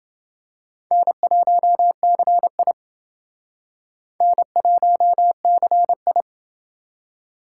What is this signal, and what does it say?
Morse code, heard as a keyed tone, about 29 wpm, 700 Hz. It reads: N1CI N1CI